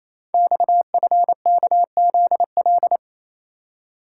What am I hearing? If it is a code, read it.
Morse code, heard as a keyed tone, about 28 wpm, 700 Hz. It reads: XFKZL
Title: Code Groups - 5 Letters - 28wpm